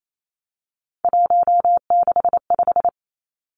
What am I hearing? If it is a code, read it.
Morse code, heard as a keyed tone, about 28 wpm, 700 Hz. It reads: 165
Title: Three Challenging Letter-Number Combinations 28wpm